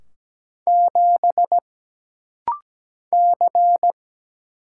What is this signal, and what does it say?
Morse code, heard as a keyed tone, about 17 wpm, 700 Hz. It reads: 7 C
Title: Single Letter-Number Alt - Rapid-Fire 17wpm